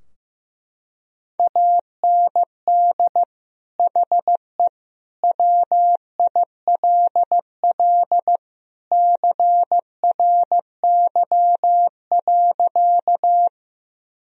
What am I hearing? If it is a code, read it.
Morse code, heard as a keyed tone, about 15 wpm, 700 Hz. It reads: AND HE WILL CRY.
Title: Sentences from Top 1000 Words - Review Part 2 - 15wpm